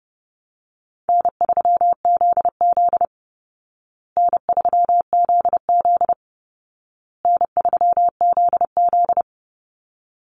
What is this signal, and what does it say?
Morse code, heard as a keyed tone, about 30 wpm, 700 Hz. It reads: N3ZZ N3ZZ N3ZZ
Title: Easy Call Signs Repeated 3x - 30wpm